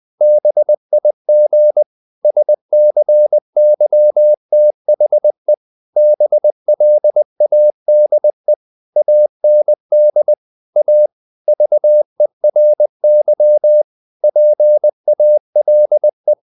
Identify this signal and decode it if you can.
Morse code, heard as a keyed tone, about 20 wpm, 600 Hz. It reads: BIG SCYTHE BLADE AND A VERY PALE
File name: the old man and the sea part 2 20wpm